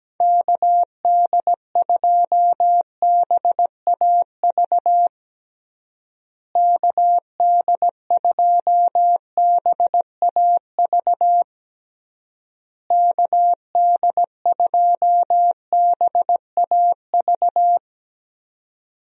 Morse code, heard as a keyed tone, about 17 wpm, 700 Hz. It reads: KD2BAV KD2BAV KD2BAV
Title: USA Call Signs - 6 Characters - Repeated 3x - 17wpm